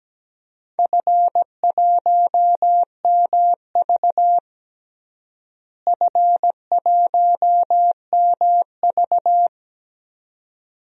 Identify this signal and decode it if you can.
Morse code, heard as a keyed tone, about 17 wpm, 700 Hz. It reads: F1MV F1MV
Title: DXCC-100 Call Signs - 4 Characters - Repeated 2x - 17wpm